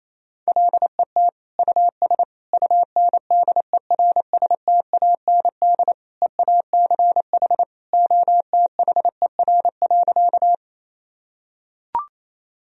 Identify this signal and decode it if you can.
Morse code, heard as a keyed tone, about 28 wpm, 700 Hz. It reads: LET US UNDERSTAND EACH OTHER.
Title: Sentences from Top 1000 Words - Review Part 1 - 28wpm